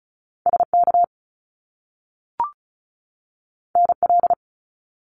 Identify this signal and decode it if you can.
Morse code, heard as a keyed tone, about 36 wpm, 700 Hz. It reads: SK NL
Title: Canadian Provinces and Territories 35wpm